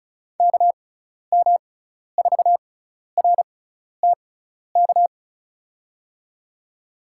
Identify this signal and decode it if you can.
Morse code, heard as a keyed tone, about 36 wpm, spaced as for 12 wpm, 700 Hz. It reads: KM4RTK
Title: USA Call Signs - 6x Character Spacing - 35wpm